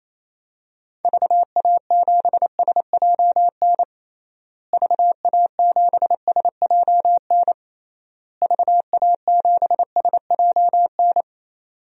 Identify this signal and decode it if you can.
Morse code, heard as a keyed tone, about 28 wpm, 700 Hz. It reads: VA7SJN VA7SJN VA7SJN
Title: DXCC-100 Call Signs - 6 Characters - Repeated 3x - 28wpm